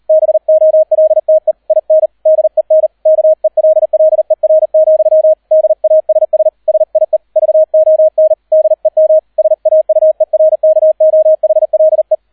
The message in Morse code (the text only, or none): BOLTE IN DEN KELLER, DASS SIE VON DEM SAUERKOHLE